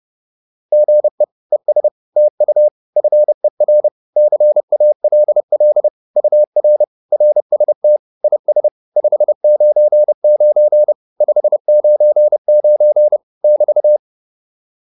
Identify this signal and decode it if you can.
Morse code, heard as a keyed tone, about 30 wpm, 600 Hz. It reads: GE ES TU FER CALL UR RST IS 599 599 <BT>
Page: advanced-practice-files